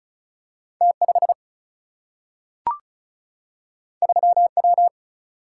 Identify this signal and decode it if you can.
Morse code, heard as a keyed tone, about 35 wpm, 700 Hz. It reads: T5 3W
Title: Single-Letter Single-Number 35wpm